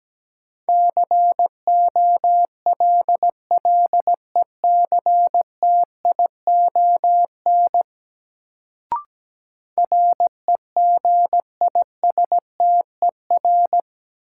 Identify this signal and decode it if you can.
Morse code, heard as a keyed tone, about 17 wpm, 700 Hz. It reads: COLLECTION REGISTER
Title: Sentences from Top 1200 Words - New Word Drill Alt - 17wpm